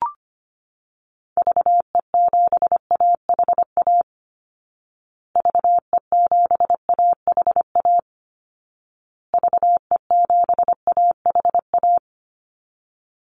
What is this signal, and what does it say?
Morse code, heard as a keyed tone, about 25 wpm, 700 Hz. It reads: VE7AHA VE7AHA VE7AHA
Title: Call Signs Repeated 2x - 25wpm